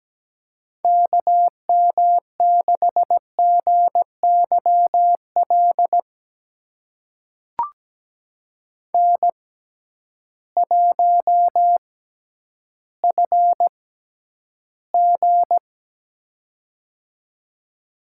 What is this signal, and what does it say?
Morse code, heard as a keyed tone, about 17 wpm, spaced as for 10 wpm, 700 Hz. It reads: KM6GYL N1FG
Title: USA Call Signs - 6x Character Spacing - 17wpm